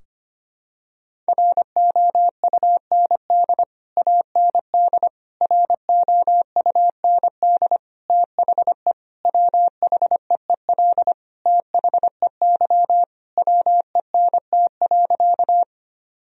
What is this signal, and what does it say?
Morse code, heard as a keyed tone, about 25 wpm, 700 Hz. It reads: ROUND AND ROUND THE WHEEL THEY WENT.
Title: Sentences from Top 1000 Words - Review Part 3 - 25wpm